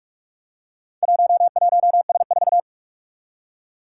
Morse code, heard as a keyed tone, about 44 wpm, 700 Hz. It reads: 11S4
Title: Four Challenging Letter-Number Combinations 45wpm